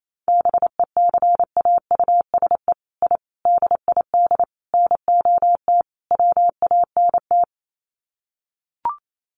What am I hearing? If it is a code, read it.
Morse code, heard as a keyed tone, about 28 wpm, 700 Hz. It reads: BECAUSE I DID NOT WANT
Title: Sets of 5 Words Encore2 28wpm